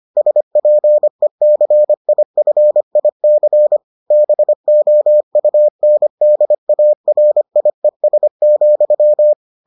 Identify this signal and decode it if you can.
Morse code, heard as a keyed tone, about 25 wpm, 600 Hz. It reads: SPECIFIC BOUNDARIES,